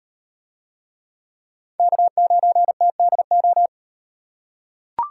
Morse code, heard as a keyed tone, about 34 wpm, 700 Hz. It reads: K9TDO